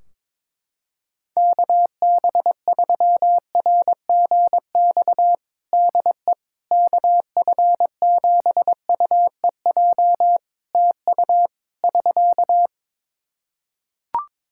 Morse code, heard as a keyed tone, about 22 wpm, 700 Hz. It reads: KB3RGX DE KF7UEJ TU <SK>
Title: Practice QSO - Exchange 4 - Ending - 22wpm